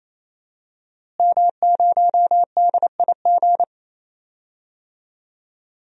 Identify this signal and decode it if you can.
Morse code, heard as a keyed tone, about 28 wpm, 700 Hz. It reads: M0DIG
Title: DXCC-100 Call Signs - 5 Characters - Repeated 3x - 28wpm